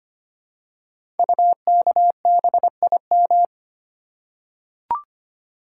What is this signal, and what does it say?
Morse code, heard as a keyed tone, about 25 wpm, 700 Hz. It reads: UKBIM
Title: Code Groups - 5 Letters - 25wpm